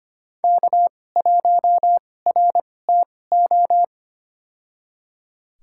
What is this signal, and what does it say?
Morse code, heard as a keyed tone, about 25 wpm, spaced as for 18 wpm, 700 Hz. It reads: K1RTO
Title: USA Call Signs - 2x Character Spacing - 25wpm